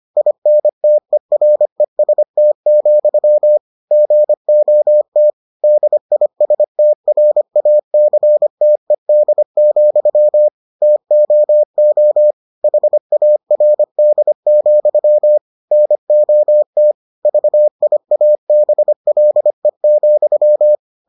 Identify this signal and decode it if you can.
Morse code, heard as a keyed tone, about 25 wpm, 600 Hz. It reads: INTEREST, GOT DISTRACTED, TOO HARD, NOT VIABLE,